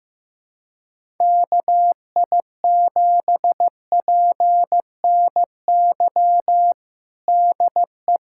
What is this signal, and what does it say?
Morse code, heard as a keyed tone, about 15 wpm, 700 Hz. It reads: KI7PNY DE
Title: USA Callsign DE Callsign - 6 Characters - 15wpm